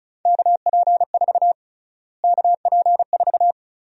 Morse code, heard as a keyed tone, about 36 wpm, 700 Hz. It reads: KP4 KP4
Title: USA Call Sign Prefixes - Repeated 2x - 35wpm